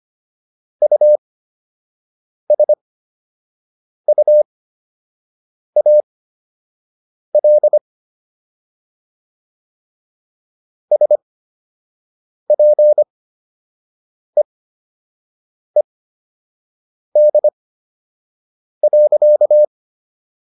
Morse code, heard as a keyed tone, about 25 wpm, spaced as for 6 wpm, 600 Hz. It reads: USUAL SPEED.